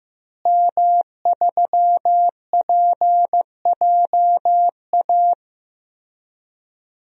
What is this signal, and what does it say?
Morse code, heard as a keyed tone, about 16 wpm, 700 Hz. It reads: M3PJA
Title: DXCC-100 Call Signs - 5 Characters - Repeated 2x - 15wpm